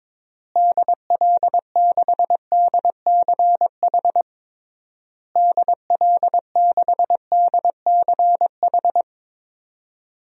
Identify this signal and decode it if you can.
Morse code, heard as a keyed tone, about 22 wpm, 700 Hz. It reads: DL6DCH DL6DCH
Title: DXCC-100 Call Signs - 6 Characters - Repeated 3x - 22wpm